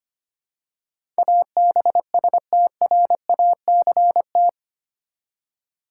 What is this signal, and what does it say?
Morse code, heard as a keyed tone, about 25 wpm, 700 Hz. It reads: ABSTRACT
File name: Eight Letter Words - 25wpm